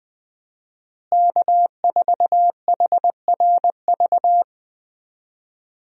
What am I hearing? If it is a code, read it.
Morse code, heard as a keyed tone, about 20 wpm, 700 Hz. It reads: K4HRV
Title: USA Call Signs - 5 Characters - Repeated 2x - 20wpm